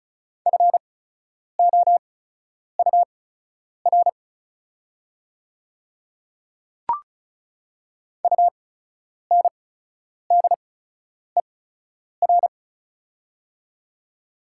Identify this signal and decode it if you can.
Morse code, heard as a keyed tone, about 36 wpm, spaced as for 10 wpm, 700 Hz. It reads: FOUR UNDER
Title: 8x Character Spacing - Top 500 Words - 35wpm